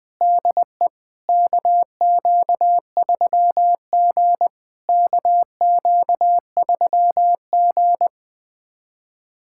Morse code, heard as a keyed tone, about 20 wpm, 700 Hz. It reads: DE KQ3G KQ3G
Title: SOTA QSO Practice 20wpm